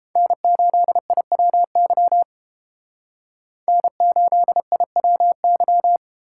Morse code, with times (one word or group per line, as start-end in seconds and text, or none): 0.2-2.2 N8IWY
3.7-6.0 N8IWY